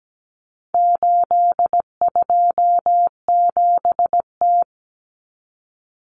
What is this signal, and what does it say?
Morse code, heard as a keyed tone, about 17 wpm, 700 Hz. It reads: 827T
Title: Common Cut Numbers - 4-Digit 17wpm